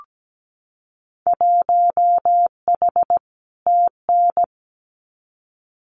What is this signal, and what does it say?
Morse code, heard as a keyed tone, about 17 wpm, 700 Hz. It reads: 1H TN
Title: Winter Field Day Exchanges 17wpm